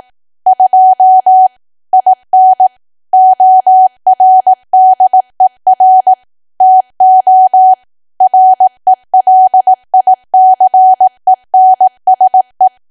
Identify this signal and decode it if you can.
Morse code, heard as a keyed tone, about 18 wpm, 750 Hz. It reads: 2 IN ORDER TO RELICENSE
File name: cwa / W1AW-Level2 / 18WPM